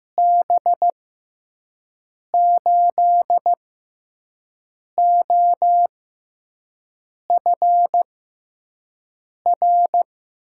Morse code, heard as a keyed tone, about 15 wpm, spaced as for 5 wpm, 700 Hz. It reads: B8OFR
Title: USA Call Signs - 6 Characters - 6x Character Spacing - 15wpm